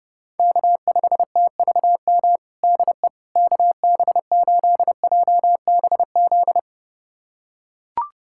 Morse code, heard as a keyed tone, about 30 wpm, 700 Hz. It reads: K5TVM DE KB8JBZ
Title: USA Callsign DE Callsign 30wpm